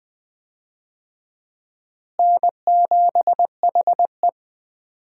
Morse code, heard as a keyed tone, about 20 wpm, 700 Hz. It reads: N7HE
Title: USA Call Signs - 4 Characters - Repeated 3x - 20wpm